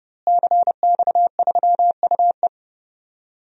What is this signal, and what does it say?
Morse code, heard as a keyed tone, about 30 wpm, 700 Hz. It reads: CX3UE
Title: DXCC-100 Call Signs - 5 Characters - Repeated 3x - 30wpm